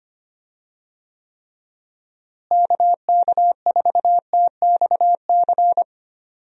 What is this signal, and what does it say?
Morse code, heard as a keyed tone, about 25 wpm, 700 Hz. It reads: KK4TXC
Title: USA Call Signs - 6 Characters - Repeated 3x - 25wpm